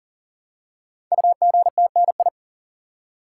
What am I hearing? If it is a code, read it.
Morse code, heard as a keyed tone, about 42 wpm, 690 Hz. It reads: UGTNI
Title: Code Groups - 5 Letters - 40wpm